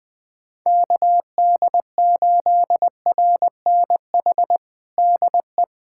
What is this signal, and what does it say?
Morse code, heard as a keyed tone, about 21 wpm, 700 Hz. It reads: KD8RNH DE
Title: USA Callsign DE Callsign 20wpm